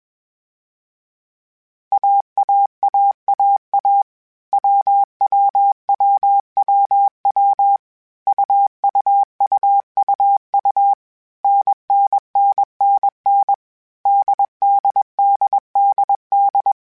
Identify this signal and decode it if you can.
Morse code, heard as a keyed tone, about 21 wpm, 800 Hz. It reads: AAAAA WWWWW UUUUU NNNNN DDDDD
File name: Groups@20 / Groups@20 mp3s / AWUNDG@20